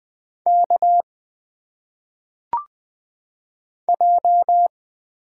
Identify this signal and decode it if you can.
Morse code, heard as a keyed tone, about 20 wpm, 700 Hz. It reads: K J
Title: Single Letters - Phonetic Pronunciation 20wpm